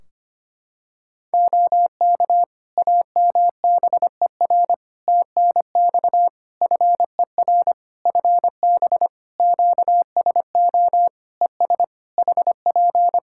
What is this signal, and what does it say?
Morse code, heard as a keyed tone, about 25 wpm, 700 Hz. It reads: OK AMBER TNX FER FB QSO ES HP
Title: Practice QSO - Exchange 4 - Ending - 25wpm